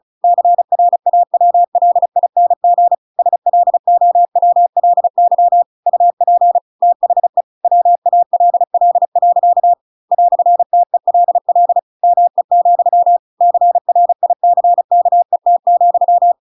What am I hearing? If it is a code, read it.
Morse code, heard as a keyed tone, about 34 wpm, 700 Hz. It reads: CRAWLING SLOWLY UP THE WALL. "TELL ME, CRICKET,